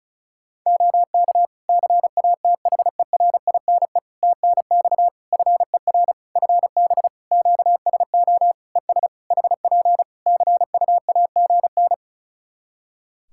Morse code, heard as a keyed tone, about 36 wpm, 700 Hz. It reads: OK CATHERINE TNX FER FB QSO ES HP CUAGN
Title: Practice QSO - Exchange 4 - Ending - 35wpm